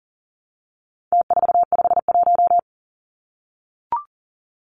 Morse code, heard as a keyed tone, about 40 wpm, 700 Hz. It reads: T451
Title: Numbers - Contest Style - 4-Digit 40wpm